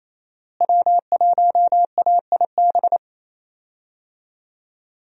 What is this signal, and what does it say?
Morse code, heard as a keyed tone, about 28 wpm, 700 Hz. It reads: W1AIB
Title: USA Call Signs - 5 Characters - Repeated 2x - 28wpm